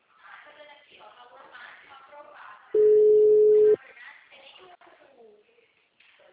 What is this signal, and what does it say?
Ringback tone (Europe, 425 Hz), with speech.